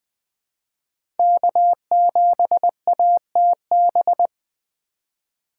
Morse code, heard as a keyed tone, about 20 wpm, 700 Hz. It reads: K7ATB
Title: USA Call Signs - 2x Character Spacing - 20wpm